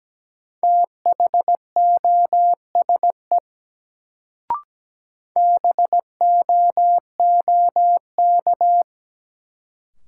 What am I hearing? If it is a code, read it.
Morse code, heard as a keyed tone, about 17 wpm, 700 Hz. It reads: THOSE BOOK
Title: Sentences from Top 300 Words - New Word Drill - 17wpm